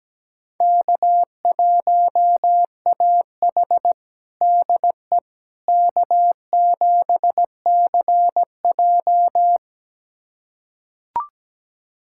Morse code, heard as a keyed tone, about 17 wpm, 700 Hz. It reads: K1AH DE K7CJ
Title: USA Callsign DE Callsign - 4 Characters - 17wpm